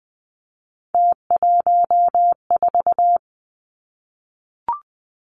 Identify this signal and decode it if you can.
Morse code, heard as a keyed tone, about 19 wpm, 700 Hz. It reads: T14
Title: Three Challenging Letter-Number Combinations 20wpm